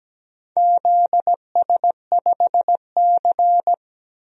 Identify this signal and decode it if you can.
Morse code, heard as a keyed tone, about 17 wpm, 700 Hz. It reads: ZS5C
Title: DXCC-100 Call Signs - 4 Characters - 17wpm